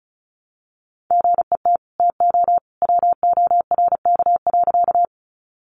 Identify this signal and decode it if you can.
Morse code, heard as a keyed tone, about 34 wpm, 700 Hz. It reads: GET TO WORK.
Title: Sentences from Top 200 Words 35wpm